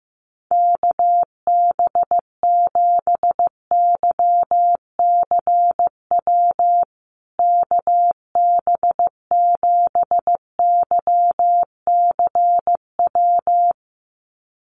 Morse code, heard as a keyed tone, about 15 wpm, 700 Hz. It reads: KB7YCW KB7YCW
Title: USA Call Signs Repeated 2x 15wpm